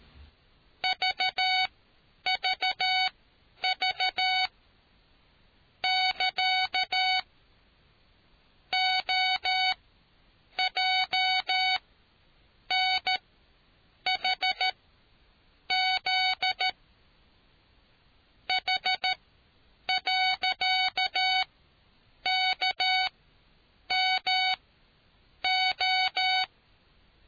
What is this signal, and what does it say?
Morse code, heard as a keyed tone, about 13 wpm, spaced as for 7 wpm, 750 Hz. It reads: VVV <KA> OJNHZ H.KMO